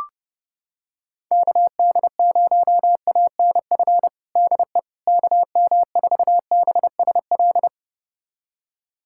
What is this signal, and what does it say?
Morse code, heard as a keyed tone, about 32 wpm, 700 Hz. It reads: KD0ANF DE KM4BSL
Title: USA Callsign DE Callsign - 6 Characters - 30wpm